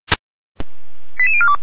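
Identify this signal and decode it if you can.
Five-tone selective call (ZVEI), heard as quick runs of five tones, tones of about 70 ms. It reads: 88042
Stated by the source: Five-tone ZVEI selective call 89401 received on SDR.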